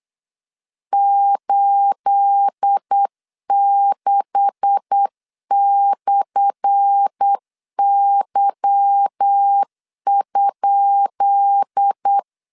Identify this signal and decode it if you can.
Morse code, heard as a keyed tone, about 8 wpm, 800 Hz. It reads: 86/Y?
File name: LEZIONI / 10 SIGNS / 10B INT 8wpm 3534